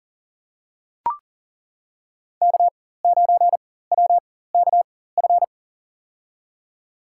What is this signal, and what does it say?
Morse code, heard as a keyed tone, about 37 wpm, spaced as for 18 wpm, 700 Hz. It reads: K9WKF
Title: USA Call Signs - 5 Characters - 4x Character Spacing - 40wpm